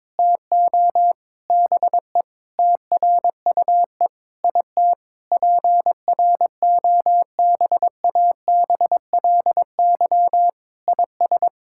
Morse code, heard as a keyed tone, about 22 wpm, 700 Hz. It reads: TO BE TRUE IT PROBABLY IS
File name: Sets of 9 Words - 22wpm